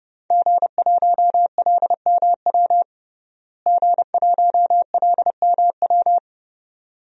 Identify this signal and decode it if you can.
Morse code, heard as a keyed tone, about 30 wpm, 700 Hz. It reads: G1LMW G1LMW
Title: DXCC-100 Call Signs - 5 Characters - Repeated 2x - 30wpm